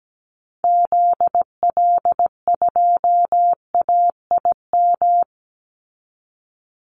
Morse code, heard as a keyed tone, about 17 wpm, 700 Hz. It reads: ZL2AIM
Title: Call Signs Repeated 2x - 17wpm